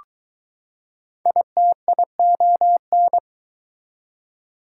Morse code, heard as a keyed tone, about 20 wpm, 700 Hz. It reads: ITION